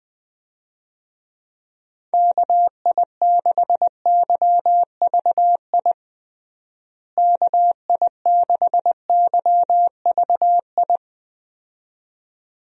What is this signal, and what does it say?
Morse code, heard as a keyed tone, about 20 wpm, 700 Hz. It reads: KI6YVI KI6YVI
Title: USA Call Signs - 6 Characters - Repeated 2x - 20wpm